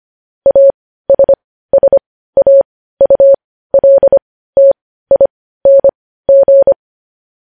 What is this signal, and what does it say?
Morse code, heard as a keyed tone, about 25 wpm, spaced as for 15 wpm, 550 Hz. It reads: ASSAULTING
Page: news-headlines-in-morse-code-at-15-wpm